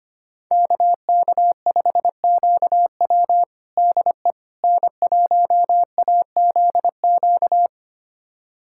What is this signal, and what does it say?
Morse code, heard as a keyed tone, about 25 wpm, 700 Hz. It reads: KK5QW DE N1AZQ
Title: USA Callsign DE Callsign - 5 Characters - 25wpm